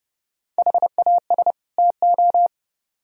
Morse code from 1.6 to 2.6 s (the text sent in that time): TO